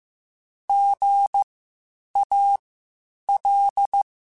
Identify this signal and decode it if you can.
Morse code, heard as a keyed tone, about 15 wpm, spaced as for 8 wpm, 800 Hz. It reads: GAL
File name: Spaced Alphas / Spaced Alphas mp3s / Gs sp @13